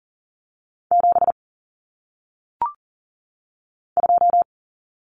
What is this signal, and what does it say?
Morse code, heard as a keyed tone, about 40 wpm, 700 Hz. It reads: 7 2
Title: Common Cut Numbers - 1-Digit 40wpm